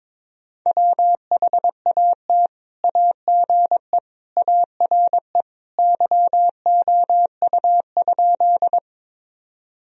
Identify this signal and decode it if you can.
Morse code, heard as a keyed tone, about 22 wpm, 700 Hz. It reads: WHAT AGE ARE YOU?
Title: Sentences from Top 1000 Words - Review Part 3 - 22wpm